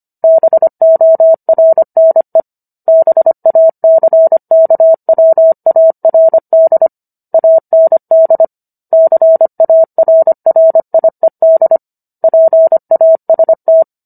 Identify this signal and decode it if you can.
Morse code, heard as a keyed tone, about 25 wpm, 650 Hz. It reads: BORNE BACKWARD AND CARRIED PAST
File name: Book One / 16-Chapter 16 - 25 WPM